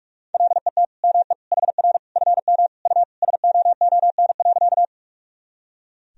Morse code, heard as a keyed tone, about 42 wpm, 700 Hz. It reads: LET ME HR FM U SOON.
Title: Sentences from Top 400 Words and Abbreviations 45wpm